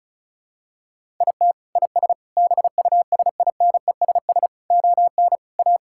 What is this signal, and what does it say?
Morse code, heard as a keyed tone, about 36 wpm, 700 Hz. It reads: IT IS BUSINESS ON A
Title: Sentences from Top 1000 Words - Review Part 5 - 35wpm